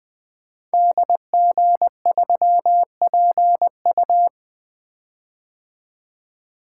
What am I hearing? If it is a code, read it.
Morse code, heard as a keyed tone, about 20 wpm, 700 Hz. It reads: DG3PU
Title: Code Groups - 5 Letter:Numbers - 20wpm